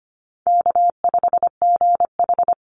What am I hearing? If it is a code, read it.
Morse code, heard as a keyed tone, about 25 wpm, 700 Hz. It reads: K5GH
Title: Call Signs Repeated 2x - 25wpm